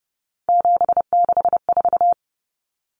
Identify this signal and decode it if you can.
Morse code, heard as a keyed tone, about 30 wpm, 700 Hz. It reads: 764
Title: Numbers - 3-Digit 30wpm